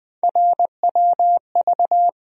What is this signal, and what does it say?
Morse code, heard as a keyed tone, about 20 wpm, 700 Hz. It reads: RWV
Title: USA Call Sign Postfixes - 3 Letters 20wpm